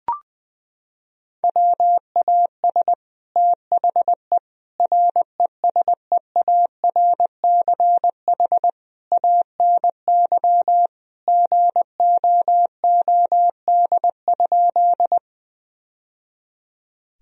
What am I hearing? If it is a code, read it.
Morse code, heard as a keyed tone, about 20 wpm, 700 Hz. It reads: WAS THE RESEARCH ANY GOOD?
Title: Sentences from Top 1000 Words - Review Part 4 - 20wpm